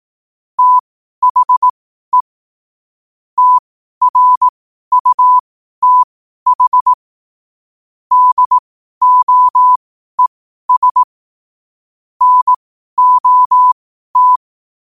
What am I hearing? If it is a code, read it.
Morse code, heard as a keyed tone, about 18 wpm, spaced as for 12 wpm, 1000 Hz. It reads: THE TRUTH DOES NOT
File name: qotdfiles / 12wpm